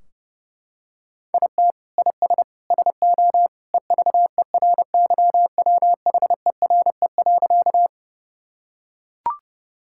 Morse code, heard as a keyed tone, about 30 wpm, 700 Hz. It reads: IT IS SO EVERYWHERE.
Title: Sentences from Top 1600 Words 30wpm